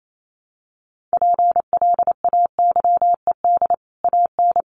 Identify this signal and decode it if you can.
Morse code, heard as a keyed tone, about 28 wpm, 700 Hz. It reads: PLAYED AN
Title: Sets of 5 Words Encore2 28wpm